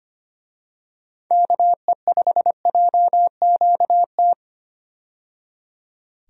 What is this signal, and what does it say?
Morse code, heard as a keyed tone, about 25 wpm, 700 Hz. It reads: KE5JQT
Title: USA Call Signs - 6 Characters - 25wpm